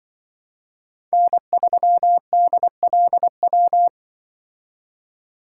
Morse code, heard as a keyed tone, about 25 wpm, 700 Hz. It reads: N3DLW
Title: Speed Racing - USA 1x3 Call Signs 20wpm